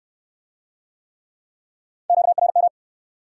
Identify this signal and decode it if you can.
Morse code, heard as a keyed tone, about 68 wpm, 700 Hz. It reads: QRM